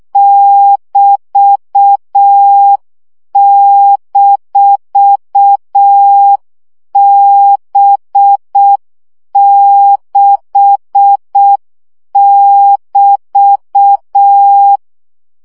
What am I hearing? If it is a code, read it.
Morse code, heard as a keyed tone, about 6 wpm, 800 Hz. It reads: <BT>-B6<BT>
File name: CW gyakorlas b 6 egyenlo minusz 5 baud